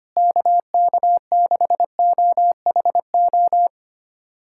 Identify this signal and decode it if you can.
Morse code, heard as a keyed tone, about 25 wpm, 700 Hz. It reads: KK6OHO
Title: USA Callsign DE Callsign 25wpm